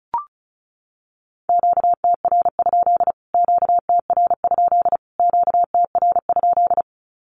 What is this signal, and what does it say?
Morse code, heard as a keyed tone, about 34 wpm, 700 Hz. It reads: QTR? QTR? QTR?
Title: Q-Codes Repeated 3x 35wpm